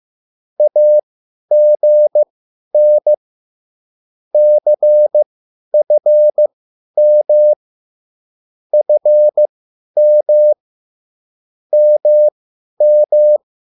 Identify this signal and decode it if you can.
Morse code, heard as a keyed tone, about 15 wpm, spaced as for 10 wpm, 600 Hz. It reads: AGN CFM FM MM
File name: cwbook